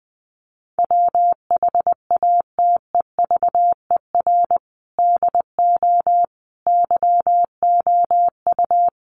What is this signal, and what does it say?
Morse code, heard as a keyed tone, about 20 wpm, 700 Hz. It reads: WHATEVER DO YOU
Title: Sentences from Top 1100 Words 20wpm